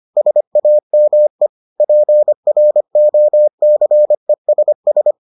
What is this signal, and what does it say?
Morse code, heard as a keyed tone, about 25 wpm, 600 Hz. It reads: SAME PROCESS